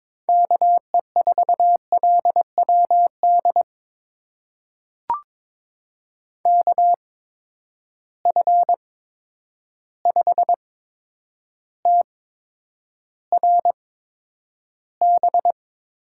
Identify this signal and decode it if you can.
Morse code, heard as a keyed tone, about 23 wpm, spaced as for 10 wpm, 700 Hz. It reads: KE4LWD KF5TRB
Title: USA Call Signs - 6 Characters - 8x Character Spacing - 22wpm